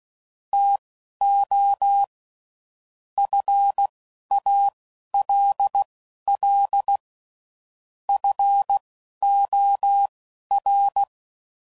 Morse code, heard as a keyed tone, about 16 wpm, spaced as for 11 wpm, 800 Hz. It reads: TO FALL FOR